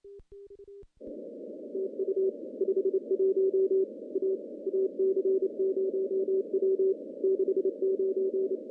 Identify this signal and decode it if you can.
Morse code, heard as a keyed tone, about 28 wpm, 380 Hz. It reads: TU 51 A AC0W 69
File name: CW-SS